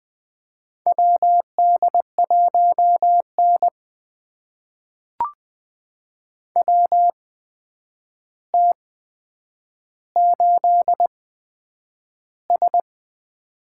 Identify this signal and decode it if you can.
Morse code, heard as a keyed tone, about 20 wpm, spaced as for 9 wpm, 700 Hz. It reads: WD1N WT8S
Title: USA Call Signs - 4 Characters - 8x Character Spacing - 20wpm